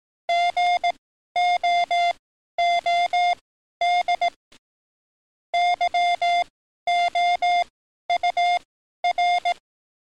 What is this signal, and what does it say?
Morse code, heard as a keyed tone, about 18 wpm, spaced as for 12 wpm, 700 Hz. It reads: GOOD YOUR
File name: training / NAVPERS 11049to11060 / NAVPERS 11053 RA RB